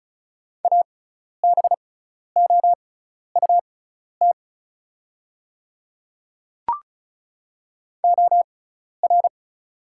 Morse code, heard as a keyed tone, about 34 wpm, spaced as for 12 wpm, 700 Hz. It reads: ABOUT OR
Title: 6x Character Spacing - Top 100 Words - 35wpm